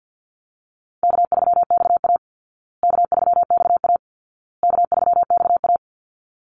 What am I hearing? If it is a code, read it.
Morse code, heard as a keyed tone, about 48 wpm, 700 Hz. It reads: K3XA K3XA K3XA
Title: Easy Call Signs Repeated 3x - 50wpm